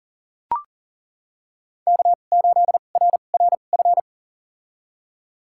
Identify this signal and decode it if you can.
Morse code, heard as a keyed tone, about 40 wpm, spaced as for 29 wpm, 700 Hz. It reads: K8RRF
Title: USA Call Signs - 5 Characters - 2x Character Spacing - 40wpm